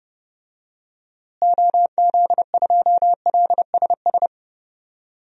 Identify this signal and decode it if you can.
Morse code, heard as a keyed tone, about 30 wpm, 700 Hz. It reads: OZ2LSS
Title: DXCC-100 Call Signs - 6 Characters - Repeated 3x - 30wpm